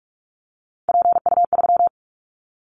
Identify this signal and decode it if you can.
Morse code, heard as a keyed tone, about 44 wpm, 700 Hz. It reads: PU3
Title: Callsign Prefixes for Top 50 DXCC - 45wpm